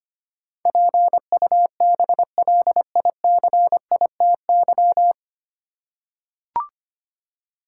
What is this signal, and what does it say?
Morse code, heard as a keyed tone, about 25 wpm, 690 Hz. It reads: PUBLICITY
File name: practice / 25wpm / 9 Letter Words - First 3 Letters Given - 25wpm